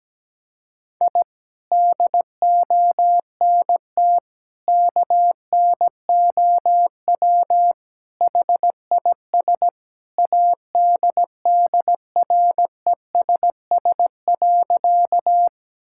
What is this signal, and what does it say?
Morse code, heard as a keyed tone, about 17 wpm, 700 Hz. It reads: I DONT KNOW HIS ADDRESS.
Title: Sentences from Top 1000 Words - Review Part 6 - 17wpm